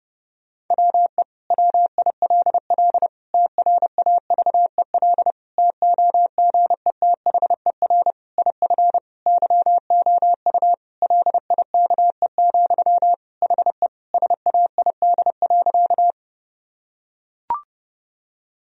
Morse code, heard as a keyed tone, about 30 wpm, 700 Hz. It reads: WE WILL TRAVEL TOGETHER IF YOU LIKE, HE SAID.
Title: Sentences from Top 1000 Words - Review Part 2 - 30wpm